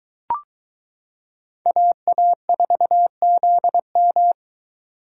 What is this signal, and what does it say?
Morse code, heard as a keyed tone, about 22 wpm, 700 Hz. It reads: AA4ZM